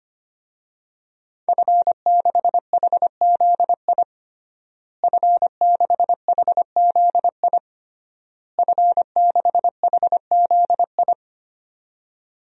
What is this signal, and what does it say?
Morse code, heard as a keyed tone, about 25 wpm, 700 Hz. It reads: F6HZI F6HZI F6HZI
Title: DXCC-100 Call Signs - 5 Characters - Repeated 3x - 25wpm